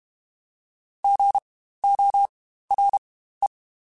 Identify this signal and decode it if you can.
Morse code, heard as a keyed tone, about 32 wpm, spaced as for 15 wpm, 800 Hz. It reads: GORE
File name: Spaced Alphas / Spaced Alphas mp3s / Gs sp @30